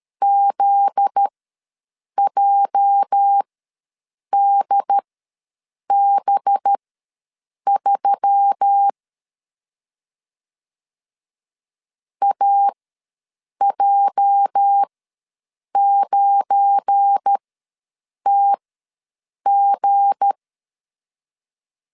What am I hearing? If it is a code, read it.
Morse code, heard as a keyed tone, about 13 wpm, spaced as for 7 wpm, 800 Hz. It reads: ZJDB3 AJ9TG
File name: LEZIONI / 8 YJ45 / 8 YJ ws 12wpm 3342